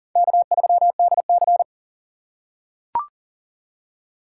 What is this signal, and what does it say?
Morse code, heard as a keyed tone, about 40 wpm, 700 Hz. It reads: K3DC
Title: USA Call Signs - 4 Characters - 4x Character Spacing - 40wpm